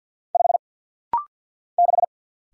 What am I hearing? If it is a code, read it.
Morse code, heard as a keyed tone, about 50 wpm, 700 Hz. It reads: 5 6
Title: Single Letter-Number - Rapid-Fire 50wpm